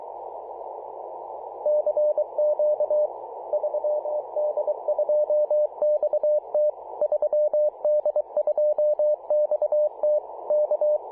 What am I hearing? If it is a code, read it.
Morse code, heard as a keyed tone, about 23 wpm, 600 Hz. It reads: CQ 3D2XT 3D2XT K
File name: posts / apr-13-2119z-3d2xt-21-021mhz-cw